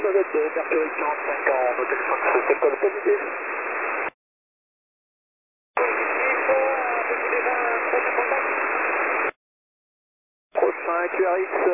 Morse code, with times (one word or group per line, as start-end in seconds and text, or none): none